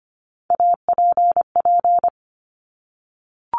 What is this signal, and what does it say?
Morse code, heard as a keyed tone, about 25 wpm, 700 Hz. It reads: APP
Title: Top Three Letter Combinations in Words 25wpm